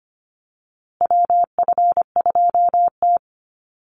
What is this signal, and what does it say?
Morse code, heard as a keyed tone, about 25 wpm, 700 Hz. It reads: WF2T
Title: USA Call Signs Repeated 2x 25wpm